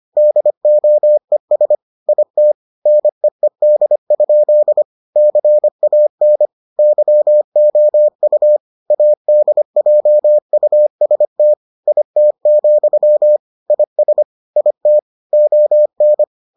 Morse code, heard as a keyed tone, about 25 wpm, 600 Hz. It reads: DOES IT NEED? CAN YOU ADJUST IT, IS IT ON